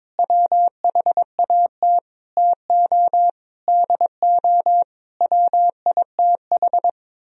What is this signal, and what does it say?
Morse code, heard as a keyed tone, about 22 wpm, 700 Hz. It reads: WHAT TO DO WITH
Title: Sentences from Top 1000 Words - Review Part 2 - 22wpm